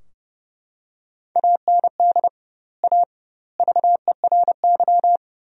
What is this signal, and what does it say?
Morse code, heard as a keyed tone, about 30 wpm, 700 Hz. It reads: AND A VERY
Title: 2x Word-Spacing - Sentences from Top 200 Words - 30wpm